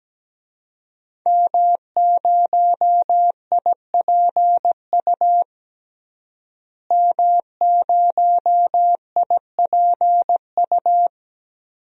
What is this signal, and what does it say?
Morse code, heard as a keyed tone, about 17 wpm, 700 Hz. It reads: M0IPU M0IPU
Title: DXCC-100 Call Signs - 5 Characters - Repeated 2x - 17wpm